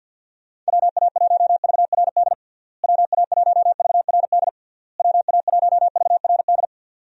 Morse code, heard as a keyed tone, about 48 wpm, 700 Hz. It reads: WA1VRD WA1VRD WA1VRD
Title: USA Call Signs - 6 Characters - Repeated 3x - 50wpm